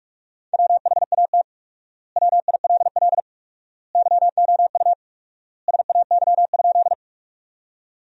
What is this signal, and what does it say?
Morse code, heard as a keyed tone, about 45 wpm, 700 Hz. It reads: WHAT WILL YOU SAY?
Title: 4x Word-Spacing - Sentences from Top 200 Words - 45wpm